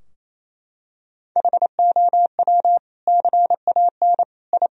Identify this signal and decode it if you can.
Morse code, heard as a keyed tone, about 28 wpm, 700 Hz. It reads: HOW CAN I
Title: Sentences from Top 1000 Words - Review Part 2 - 28wpm